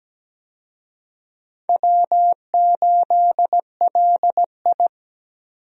Morse code, heard as a keyed tone, about 17 wpm, 700 Hz. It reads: W8LI
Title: USA Call Signs - 4 Characters - 17wpm